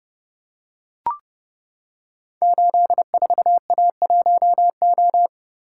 Morse code, heard as a keyed tone, about 30 wpm, 700 Hz. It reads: 84A1O
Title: Code Groups - 5 Letter:Numbers - 30wpm